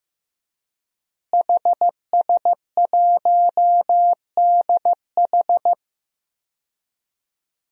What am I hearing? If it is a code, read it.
Morse code, heard as a keyed tone, about 15 wpm, 700 Hz. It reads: HS1DH
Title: DXCC-100 Call Signs - 5 Characters - Repeated 2x - 15wpm